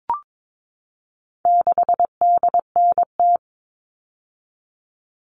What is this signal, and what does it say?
Morse code, heard as a keyed tone, about 22 wpm, 700 Hz. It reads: 6DNT
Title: All Cut Numbers - 4-Digit 22wpm